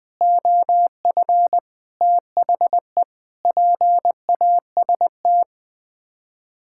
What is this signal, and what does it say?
Morse code, heard as a keyed tone, about 21 wpm, 700 Hz. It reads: OF THE PAST
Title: Sets of 6 Words - 20wpm